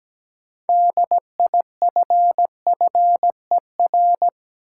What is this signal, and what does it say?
Morse code, heard as a keyed tone, about 17 wpm, 700 Hz. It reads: DIFFER
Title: Sentences from Top 1000 Words - Review Part 1 - New Word Drill - 17wpm